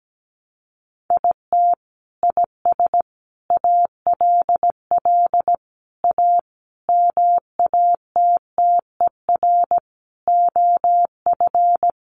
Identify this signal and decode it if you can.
Morse code, heard as a keyed tone, about 17 wpm, 700 Hz. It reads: IT IS ALL A MATTER OF
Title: Sentences from Top 1300 Words 17wpm